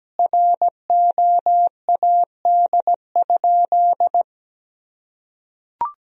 Morse code, heard as a keyed tone, about 17 wpm, 700 Hz. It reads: ROAD?
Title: Sentences from Top 1000 Words - Review Part 2 - 17wpm